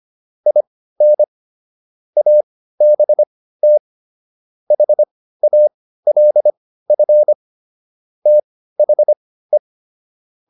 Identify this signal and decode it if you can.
Morse code, heard as a keyed tone, about 25 wpm, spaced as for 15 wpm, 600 Hz. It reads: IN ABT HALF THE